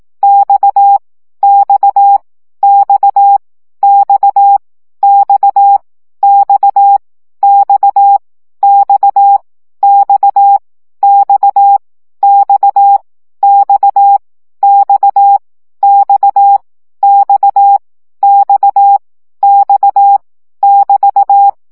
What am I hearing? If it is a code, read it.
Morse code, heard as a keyed tone, about 18 wpm, 800 Hz. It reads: X X X X X X X X X X X X X X X X X <BT>
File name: CW gyakorlas x 15 baud